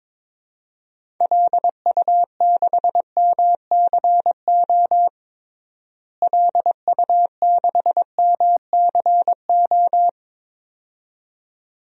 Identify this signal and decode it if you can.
Morse code, heard as a keyed tone, about 22 wpm, 700 Hz. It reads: LU6MCO LU6MCO
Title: DXCC-100 Call Signs - 6 Characters - Repeated 2x - 22wpm